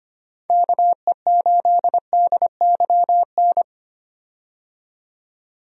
Morse code, heard as a keyed tone, about 25 wpm, 700 Hz. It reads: KE8DYN
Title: USA Call Signs - 6 Characters - 25wpm